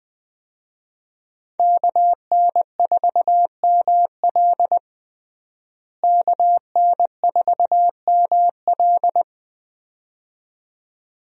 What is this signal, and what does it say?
Morse code, heard as a keyed tone, about 20 wpm, 700 Hz. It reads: KN4ML KN4ML
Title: USA Call Signs - 5 Characters - Repeated 2x - 20wpm